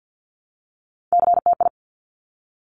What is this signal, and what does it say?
Morse code, heard as a keyed tone, about 50 wpm, 700 Hz. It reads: CTI
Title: Top Three Letter Combinations in Words 50wpm